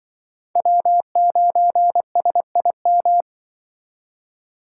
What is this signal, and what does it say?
Morse code, heard as a keyed tone, about 24 wpm, 700 Hz. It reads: W9SIM
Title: Speed Racing - USA 1x3 Call Signs 20wpm